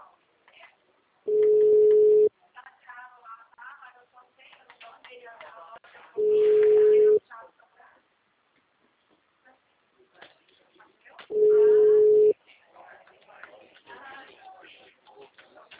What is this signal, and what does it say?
Ringback tone (Europe, 425 Hz), with speech.
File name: out- -2006- - - .18370